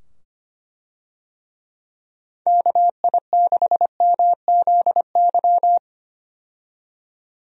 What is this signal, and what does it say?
Morse code, heard as a keyed tone, about 25 wpm, 700 Hz. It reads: KI6MZY
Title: USA Call Signs - 6 Characters - 25wpm